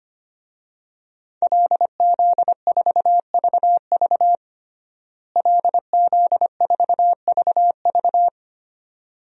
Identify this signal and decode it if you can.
Morse code, heard as a keyed tone, about 25 wpm, 700 Hz. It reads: LZ4VV LZ4VV
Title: DXCC-100 Call Signs - 5 Characters - Repeated 2x - 25wpm